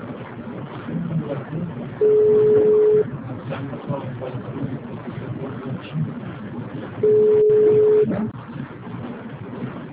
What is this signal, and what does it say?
Ringback tone (Europe, 425 Hz).
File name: out- -2031- - - .198